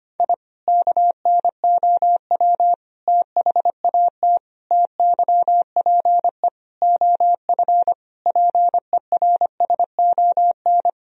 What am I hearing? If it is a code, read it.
Morse code, heard as a keyed tone, about 25 wpm, 700 Hz. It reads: I KNOW THAT TYPE OF PERSON
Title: Sentences from Top 1000 Words - Review Part 3 - 25wpm